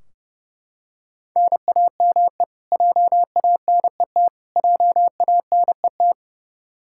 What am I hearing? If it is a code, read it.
Morse code, heard as a keyed tone, about 30 wpm, 700 Hz. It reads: NAME JANET JANET
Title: Practice QSO - Exchange 1 - Beginning - 30wpm